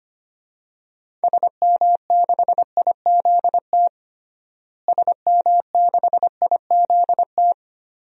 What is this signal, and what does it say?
Morse code, heard as a keyed tone, about 25 wpm, 700 Hz. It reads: SM6IZT SM6IZT
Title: DXCC-100 Call Signs - 6 Characters - Repeated 3x - 25wpm